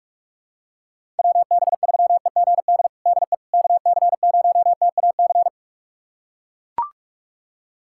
Morse code, heard as a keyed tone, about 42 wpm, 700 Hz. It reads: WB3EGD DE KC0TAC
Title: USA Callsign DE Callsign - 6 Characters - 45wpm